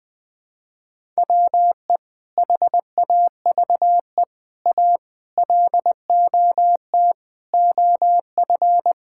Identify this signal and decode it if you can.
Morse code, heard as a keyed tone, about 19 wpm, 700 Hz. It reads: WE HAVE A LOT OF
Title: Sentences with 5-Word Phrases - Spoken Beginning - 20wpm